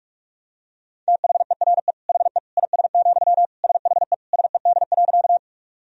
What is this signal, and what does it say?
Morse code, heard as a keyed tone, about 46 wpm, 700 Hz. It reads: THERE HE IS, SHE SED.
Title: Sentences from Top 100 Words and Abbreviations 45wpm